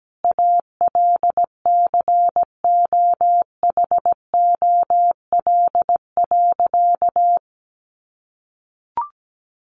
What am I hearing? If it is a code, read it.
Morse code, heard as a keyed tone, about 17 wpm, 700 Hz. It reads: ALCOHOL.
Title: Sentences from Top 1400 Words 17wpm